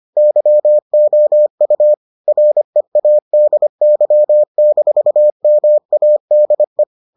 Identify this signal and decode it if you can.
Morse code, heard as a keyed tone, about 25 wpm, 600 Hz. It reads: YOU READY-MADE